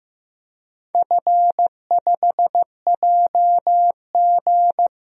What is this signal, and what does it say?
Morse code, heard as a keyed tone, about 15 wpm, 700 Hz. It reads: F5JG
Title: DXCC-100 Call Signs - 4 Characters - Repeated 2x - 15wpm